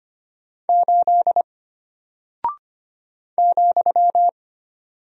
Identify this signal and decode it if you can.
Morse code, heard as a keyed tone, about 25 wpm, 700 Hz. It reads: 8 ,
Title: Single Character (Letter:Number:Punctuation:Prosign) Alt 25wpm